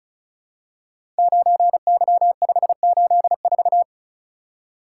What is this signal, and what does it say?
Morse code, heard as a keyed tone, about 36 wpm, 700 Hz. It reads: 9Y584
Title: Code Groups - 5 Letter:Numbers - 35wpm